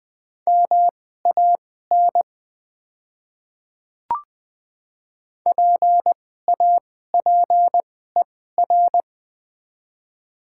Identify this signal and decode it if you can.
Morse code, heard as a keyed tone, about 20 wpm, spaced as for 15 wpm, 700 Hz. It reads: MAN PAPER
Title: 2x Character Spacing - Top 300 Words - 20wpm